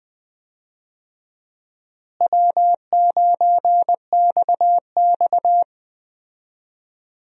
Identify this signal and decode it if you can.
Morse code, heard as a keyed tone, about 20 wpm, 700 Hz. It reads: W9XX
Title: USA Call Signs - 4 Characters - Repeated 3x - 20wpm